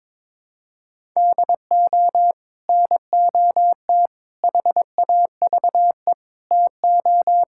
Morse code, heard as a keyed tone, about 22 wpm, 700 Hz. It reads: DO NOT HAVE TO
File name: practice / 22wpm / Build A Phrase - 5-Word Phrases - 22wpm